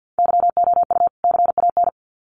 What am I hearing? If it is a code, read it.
Morse code, heard as a keyed tone, about 50 wpm, 700 Hz. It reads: YOU CAN
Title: Sets of 2 Words 50wpm